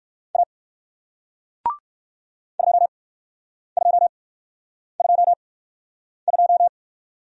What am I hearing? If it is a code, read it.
Morse code, heard as a keyed tone, about 58 wpm, spaced as for 9 wpm, 700 Hz. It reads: A 2222